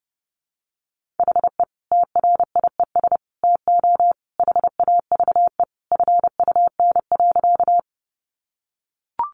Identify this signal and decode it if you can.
Morse code, heard as a keyed tone, about 30 wpm, 700 Hz. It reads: HE TRIES TO HAVE FUN.
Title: Easy Phrases 30wpm